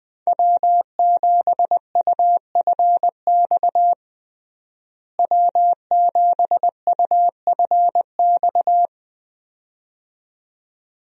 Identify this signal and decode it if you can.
Morse code, heard as a keyed tone, about 20 wpm, 700 Hz. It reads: W7UFX W7UFX
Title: USA Call Signs - 5 Characters - Repeated 2x - 20wpm